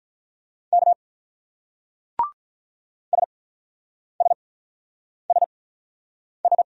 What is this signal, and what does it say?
Morse code, heard as a keyed tone, about 46 wpm, spaced as for 9 wpm, 690 Hz. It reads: K SSSS